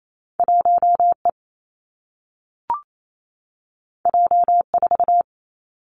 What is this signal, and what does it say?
Morse code, heard as a keyed tone, about 28 wpm, 700 Hz. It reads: 1 E J 4
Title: Two Challenging Letter-Number Combinations 28wpm